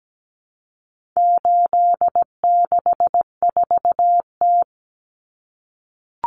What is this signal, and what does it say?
Morse code, heard as a keyed tone, about 17 wpm, 700 Hz. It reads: 864T
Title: Common Cut Numbers - 4-Digit 17wpm